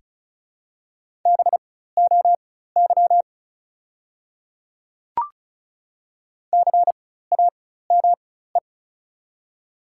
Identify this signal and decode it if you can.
Morse code, heard as a keyed tone, about 36 wpm, spaced as for 16 wpm, 700 Hz. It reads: BOY CAME
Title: 4x Character Spacing - Top 200 Words - 35wpm